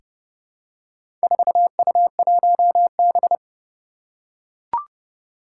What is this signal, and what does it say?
Morse code, heard as a keyed tone, about 30 wpm, 690 Hz. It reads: 4U1B
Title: Four Challenging Letter-Number Combinations 30wpm